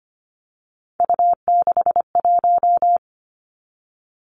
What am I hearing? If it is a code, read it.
Morse code, heard as a keyed tone, about 25 wpm, 700 Hz. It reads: U61
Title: Three Challenging Letter-Number Combinations 25wpm